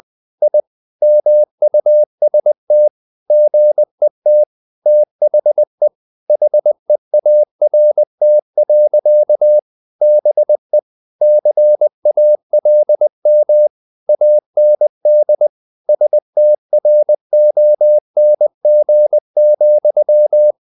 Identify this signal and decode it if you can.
Morse code, heard as a keyed tone, about 20 wpm, 600 Hz. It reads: I MUST GET THE HEART. BE CALM AND STRONG,
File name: the old man and the sea part 2 20wpm